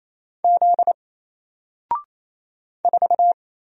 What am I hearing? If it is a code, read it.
Morse code, heard as a keyed tone, about 28 wpm, 700 Hz. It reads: Z 4
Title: Single Letter-Number 28wpm